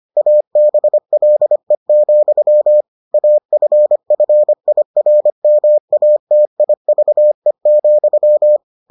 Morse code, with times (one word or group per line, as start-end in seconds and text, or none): 0.2-2.8 ABLE,
3.1-8.6 AFFIRMATIVE,